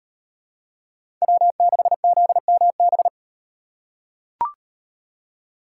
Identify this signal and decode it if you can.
Morse code, heard as a keyed tone, about 38 wpm, 700 Hz. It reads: W6ZMB